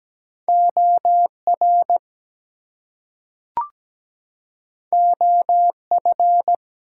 Morse code, heard as a keyed tone, about 17 wpm, 700 Hz. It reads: OR OF
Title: Two Letter Words - 17wpm